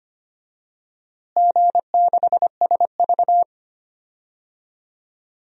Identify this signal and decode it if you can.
Morse code, heard as a keyed tone, about 25 wpm, 700 Hz. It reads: G6SV
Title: DXCC-100 Call Signs - 4 Characters - Repeated 2x - 25wpm